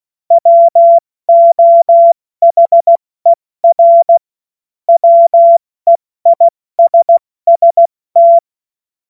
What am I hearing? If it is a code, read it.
Morse code, heard as a keyed tone, about 16 wpm, 680 Hz. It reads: WOHER WEISST
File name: Morsekurs / Witze / Witz 14. September 2021, 70 F80